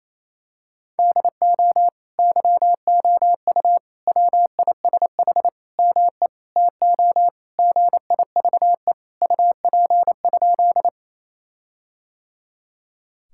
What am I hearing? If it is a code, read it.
Morse code, heard as a keyed tone, about 28 wpm, 700 Hz. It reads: DO YOU WISH ME TO GIVE UP?
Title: Sentences from Top 1600 Words 28wpm